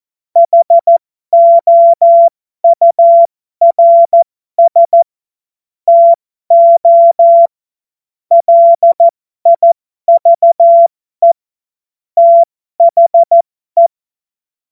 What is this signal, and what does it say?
Morse code, heard as a keyed tone, about 14 wpm, spaced as for 12 wpm, 680 Hz. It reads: HOURS TO LIVE THE
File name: Morsekurs / Witze